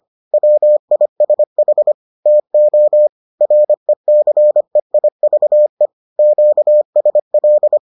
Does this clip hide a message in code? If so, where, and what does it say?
Yes: Morse code from 0.3 to 7.8 s, heard as a keyed tone, about 25 wpm, 600 Hz. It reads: WISH TO RECEIVE QSL